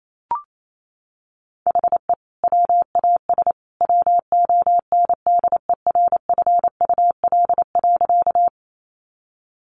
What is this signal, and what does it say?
Morse code, heard as a keyed tone, about 28 wpm, 700 Hz. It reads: HE WAS WONDERFUL.
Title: Sentences from Top 1100 Words 28wpm